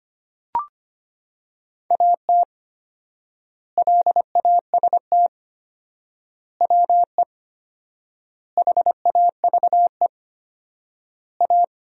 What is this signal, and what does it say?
Morse code, heard as a keyed tone, about 25 wpm, 700 Hz. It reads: AT LAST WE HAVE A
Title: 4x Word-Spacing - Sentences from Top 300 Words - 25wpm